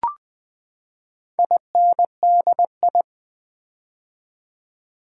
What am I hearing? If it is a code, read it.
Morse code, heard as a keyed tone, about 20 wpm, 700 Hz. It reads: INDI
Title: Build A Word - 4 Letters At A Time - 10-Letter Words - 20wpm